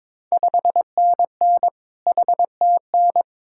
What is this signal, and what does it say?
Morse code, heard as a keyed tone, about 22 wpm, 690 Hz. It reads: 5NN HTN
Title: Ontario QSO Party Exchange 22wpm